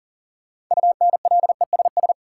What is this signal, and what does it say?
Morse code, heard as a keyed tone, about 40 wpm, 700 Hz. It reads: UNLESS
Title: Sentences from Top 1300 Words - New Word Drill Alt - 40wpm